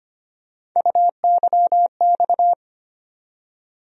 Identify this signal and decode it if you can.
Morse code, heard as a keyed tone, about 25 wpm, 700 Hz. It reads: UYX
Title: USA Call Sign Postfixes - 3 Letters 25wpm